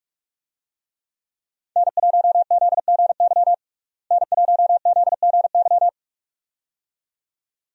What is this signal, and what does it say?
Morse code, heard as a keyed tone, about 46 wpm, 700 Hz. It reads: N1ZGY N1ZGY
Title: USA Call Signs - 5 Characters - Repeated 2x - 45wpm